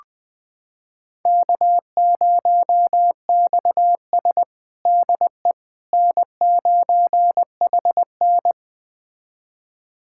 Morse code, heard as a keyed tone, about 20 wpm, 700 Hz. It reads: K0XS DE N9HN
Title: USA Callsign DE Callsign - 4 Characters - 20wpm